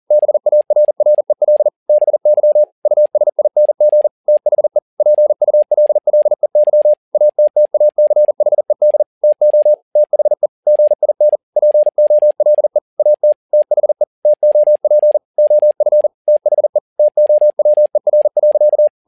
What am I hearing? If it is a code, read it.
Morse code, heard as a keyed tone, about 40 wpm, 600 Hz. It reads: BARREL BY USING THE PULLEY ATTACHED TO THE GIN POLE AT THE TOP OF THE TOWER.
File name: learning center / tower trouble / tower touble 40wpm